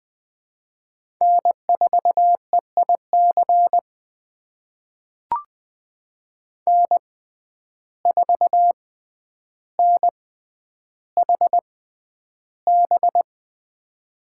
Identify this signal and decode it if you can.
Morse code, heard as a keyed tone, about 20 wpm, spaced as for 10 wpm, 700 Hz. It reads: N4EIC N4NHB
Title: USA Call Signs - 5 Characters - 6x Character Spacing - 20wpm